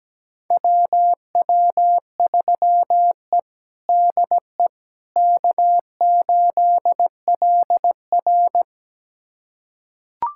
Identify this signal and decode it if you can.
Morse code, heard as a keyed tone, about 17 wpm, 700 Hz. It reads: WW3E DE K8LR
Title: USA Callsign DE Callsign - 4 Characters - 17wpm